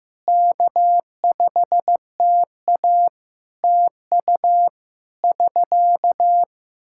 Morse code, heard as a keyed tone, about 15 wpm, 700 Hz. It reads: K5TA TU <SK>
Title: Practice QSO - Exchange 4 - Ending - 15wpm